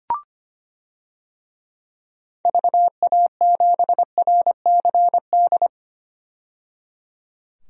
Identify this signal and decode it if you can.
Morse code, heard as a keyed tone, about 25 wpm, 700 Hz. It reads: VA7RCD
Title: DXCC-100 Call Signs - 6 Characters - 25wpm